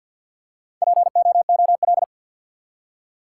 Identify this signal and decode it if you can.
Morse code, heard as a keyed tone, about 52 wpm, 700 Hz. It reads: POOL
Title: Four Letter Words - 50wpm